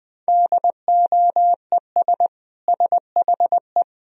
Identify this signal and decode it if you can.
Morse code, heard as a keyed tone, about 20 wpm, 700 Hz. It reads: DOES SHE
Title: Sentences from Top 1000 Words - Review Part 2 - 20wpm